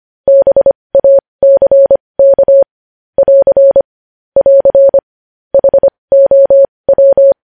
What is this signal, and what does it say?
Morse code, heard as a keyed tone, about 25 wpm, spaced as for 20 wpm, 550 Hz. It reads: BACK <AR> <AR> HOW
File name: podcast / News headlines-1x-25x20